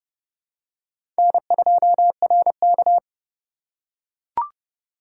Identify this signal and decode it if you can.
Morse code, heard as a keyed tone, about 29 wpm, 700 Hz. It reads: N2RK
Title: USA Call Signs - 4 Characters - 3x Character Spacing - 30wpm